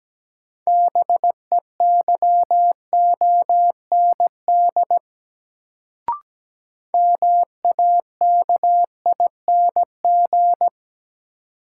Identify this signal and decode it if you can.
Morse code, heard as a keyed tone, about 17 wpm, 700 Hz. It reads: BEYOND MAKING
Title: Sentences from Top 1000 Words - Review Part 6 - New Word Drill - 17wpm